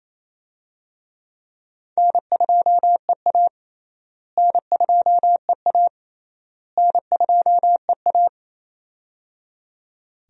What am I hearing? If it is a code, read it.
Morse code, heard as a keyed tone, about 28 wpm, 700 Hz. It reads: N2EA N2EA N2EA
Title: USA Call Signs - 4 Characters - Repeated 3x - 28wpm